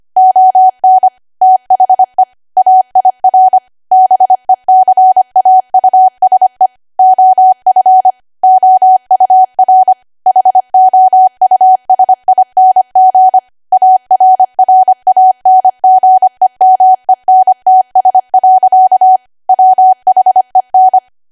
Morse code, heard as a keyed tone, about 25 wpm, 750 Hz. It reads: ON THE AIR BECAUSE OF OUR HOUSING ARRANGEMENTS. WHEN
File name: Archive / 25 WPM / 25WPM